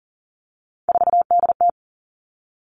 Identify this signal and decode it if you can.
Morse code, heard as a keyed tone, about 38 wpm, 700 Hz. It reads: 4DT
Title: Three Challenging Letter-Number Combinations 40wpm